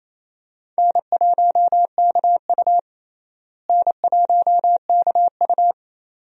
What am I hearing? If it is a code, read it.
Morse code, heard as a keyed tone, about 28 wpm, 700 Hz. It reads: N1KU N1KU
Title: USA Call Signs - 4 Characters - Repeated 2x - 28wpm